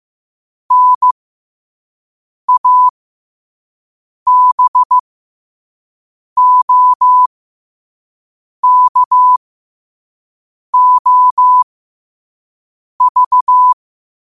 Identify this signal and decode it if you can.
Morse code, heard as a keyed tone, about 15 wpm, spaced as for 5 wpm, 1000 Hz. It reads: NABOKOV
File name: qotdfiles / 5wpm